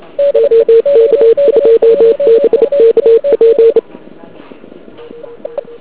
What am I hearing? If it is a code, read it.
Morse code, heard as a keyed tone, about 29 wpm, 460 Hz. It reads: OK2BKP
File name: ARCHIV / KX3 / KV PA